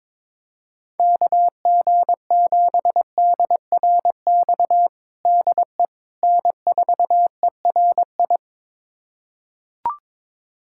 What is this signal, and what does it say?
Morse code, heard as a keyed tone, about 22 wpm, 700 Hz. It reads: KG7DRX DE N4ERI
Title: USA Callsign DE Callsign 22wpm